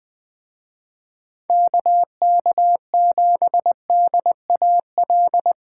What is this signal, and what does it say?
Morse code, heard as a keyed tone, about 21 wpm, 700 Hz. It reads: KK7DAL
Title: Speed Racing - USA 2x3 Call Signs 17wpm